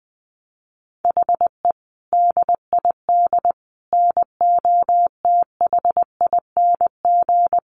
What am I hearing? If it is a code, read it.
Morse code, heard as a keyed tone, about 20 wpm, 700 Hz. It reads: HE DID NOTHING
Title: Sentences from Top 1300 Words 20wpm